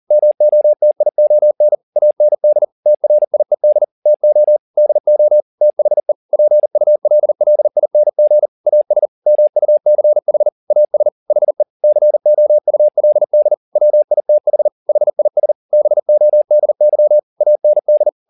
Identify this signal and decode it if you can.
Morse code, heard as a keyed tone, about 40 wpm, 600 Hz. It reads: MOTION AND TRIED TO DO THE PULLING AS MUCH AS HE COULD WITH HIS BODY AND
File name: the old man and the sea 2 40wpm